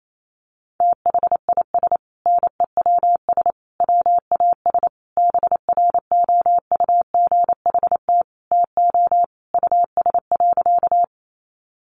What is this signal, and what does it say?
Morse code, heard as a keyed tone, about 28 wpm, 700 Hz. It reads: THIS NEWS WAS BROUGHT TO US.
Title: Sentences from Top 800 Words 28wpm